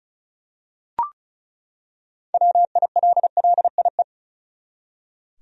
Morse code, heard as a keyed tone, about 36 wpm, 700 Hz. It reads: WILLIE
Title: USA Common Names - 35wpm